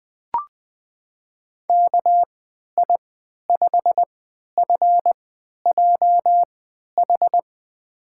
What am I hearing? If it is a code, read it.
Morse code, heard as a keyed tone, about 20 wpm, spaced as for 11 wpm, 700 Hz. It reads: KI5FJH
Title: USA Call Signs - 6 Characters - 3x Character Spacing - 20wpm